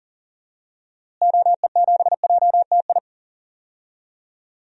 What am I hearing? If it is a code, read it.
Morse code, heard as a keyed tone, about 40 wpm, 700 Hz. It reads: OE7JTI
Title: DXCC-100 Call Signs - 6 Characters - Repeated 3x - 40wpm